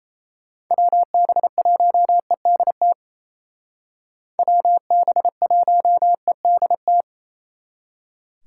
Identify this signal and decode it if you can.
Morse code, heard as a keyed tone, about 37 wpm, 700 Hz. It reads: WB1EDT WB1EDT